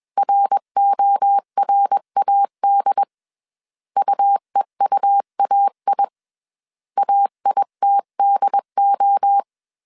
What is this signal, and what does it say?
Morse code, heard as a keyed tone, about 21 wpm, 800 Hz. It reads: RORAD UEUAI AITDO
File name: CW / DET / 1 3 1500C S B 20wpm 1754